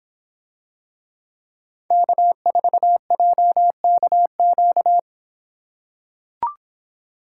Morse code, heard as a keyed tone, about 26 wpm, 700 Hz. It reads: K4JKQ